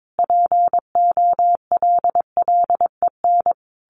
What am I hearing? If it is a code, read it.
Morse code, heard as a keyed tone, about 22 wpm, 700 Hz. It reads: POLLEN
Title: A Sound Of Thunder 22wpm